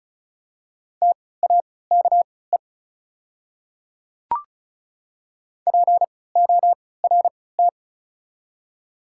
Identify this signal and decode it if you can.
Morse code, heard as a keyed tone, about 36 wpm, spaced as for 20 wpm, 700 Hz. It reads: TAKE PORT
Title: 3x Character Spacing - Top 200 Words - 35wpm